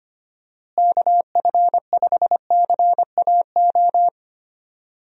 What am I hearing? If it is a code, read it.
Morse code, heard as a keyed tone, about 25 wpm, 700 Hz. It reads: KF5CAO
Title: USA Call Signs - 6 Characters - Repeated 3x - 25wpm